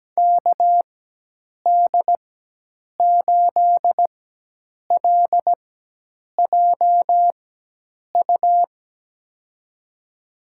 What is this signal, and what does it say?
Morse code, heard as a keyed tone, about 17 wpm, spaced as for 8 wpm, 700 Hz. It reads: KD8LJU
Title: USA Call Signs - 6 Characters - 4x Character Spacing - 17wpm